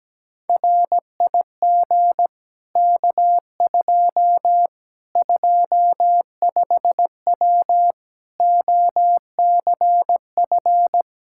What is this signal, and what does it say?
Morse code, heard as a keyed tone, about 17 wpm, 700 Hz. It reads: RIG K2 25W OCF
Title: Practice QSO - Exchange 2 - Info - 17wpm